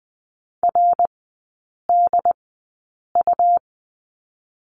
Morse code, heard as a keyed tone, about 20 wpm, spaced as for 8 wpm, 700 Hz. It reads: RDU
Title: 100 Lesson 17e - Letter u - Three Characters 20wpm